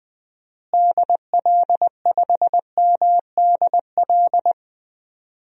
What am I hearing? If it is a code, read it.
Morse code, heard as a keyed tone, about 20 wpm, 700 Hz. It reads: DL5MDL
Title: DXCC-100 Call Signs - 6 Characters - Repeated 2x - 20wpm